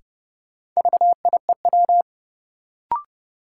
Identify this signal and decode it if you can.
Morse code, heard as a keyed tone, about 30 wpm, 700 Hz. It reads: VIEW
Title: Sentences from Top 900 Words - New Word Drill - 30wpm